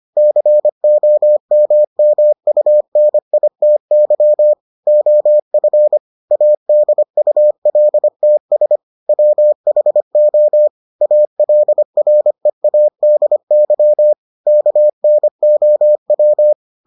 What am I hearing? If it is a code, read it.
Morse code, heard as a keyed tone, about 25 wpm, 600 Hz. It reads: COMMUNITY OF ADULTS WHO ALREADY KNOW